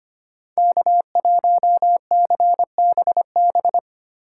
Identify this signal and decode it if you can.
Morse code, heard as a keyed tone, about 33 wpm, 700 Hz. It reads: K1CBB